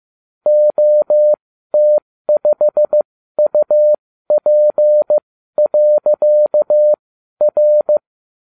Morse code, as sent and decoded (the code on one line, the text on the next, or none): --- - ..... ..- .--. .-.-.- .-.
OT5UP.R